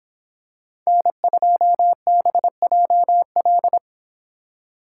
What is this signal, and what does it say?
Morse code, heard as a keyed tone, about 24 wpm, 700 Hz. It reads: N2BJL